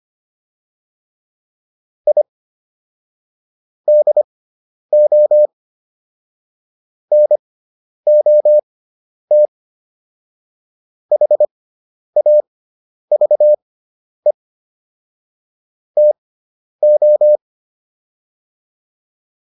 Morse code, heard as a keyed tone, about 25 wpm, spaced as for 10 wpm, 600 Hz. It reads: I DO NOT HAVE TO